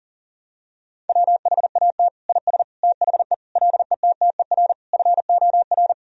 Morse code, heard as a keyed tone, about 40 wpm, 700 Hz. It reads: WHAT IS THE LETTER FOR
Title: Sentences from Top 1000 Words - Review Part 2 - 40wpm